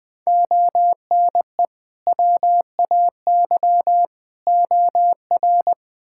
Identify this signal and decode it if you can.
Morse code, heard as a keyed tone, about 20 wpm, 700 Hz. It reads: ONE WAY OR
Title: Build A Phrase - 5-Word Phrases - 20wpm